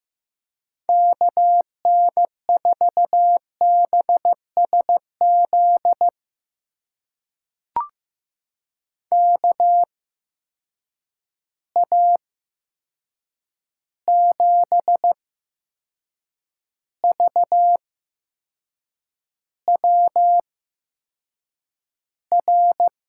Morse code, heard as a keyed tone, about 15 wpm, spaced as for 6 wpm, 700 Hz. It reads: KN4BSZ KA7VWR
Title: USA Call Signs - 6 Characters - 8x Character Spacing - 15wpm